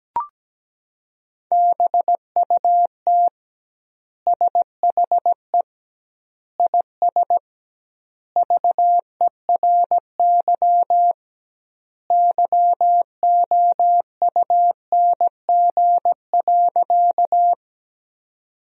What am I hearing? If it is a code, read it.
Morse code, heard as a keyed tone, about 17 wpm, 700 Hz. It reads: BUT SHE IS VERY YOUNG.
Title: 2x Word-Spacing - Sentences from Top 400 Words - 17wpm